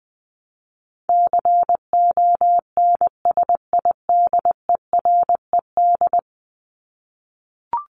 Morse code, heard as a keyed tone, about 19 wpm, 700 Hz. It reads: CONSIDERED
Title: ED Suffix - Top 100 Words with Base Words 20wpm